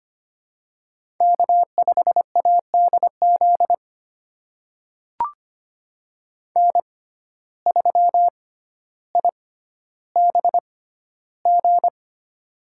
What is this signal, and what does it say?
Morse code, heard as a keyed tone, about 25 wpm, spaced as for 13 wpm, 700 Hz. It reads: K5ADZ N3IBG
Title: USA Call Signs - 5 Characters - 6x Character Spacing - 25wpm